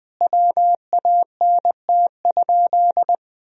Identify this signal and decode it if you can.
Morse code, heard as a keyed tone, about 20 wpm, 700 Hz. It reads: WANT?
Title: Sentences from Top 1000 Words - Review Part 4 - 20wpm